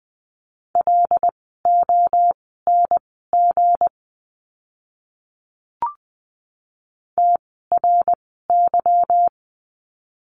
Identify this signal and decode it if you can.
Morse code, heard as a keyed tone, about 19 wpm, spaced as for 14 wpm, 700 Hz. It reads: LONG TRY
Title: 160 Lesson 26f - Letter g - Words 20wpm